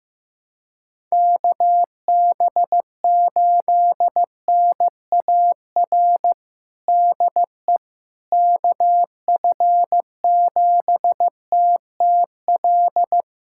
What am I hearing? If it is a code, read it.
Morse code, heard as a keyed tone, about 16 wpm, 700 Hz. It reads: KB8NAR DE KF7TTL
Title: USA Callsign DE Callsign - 6 Characters - 15wpm